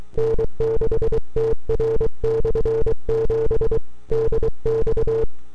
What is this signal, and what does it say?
Morse code, heard as a keyed tone, about 23 wpm, 450 Hz. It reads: N6TR/7 DX
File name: n6tr / 12Oct2009-160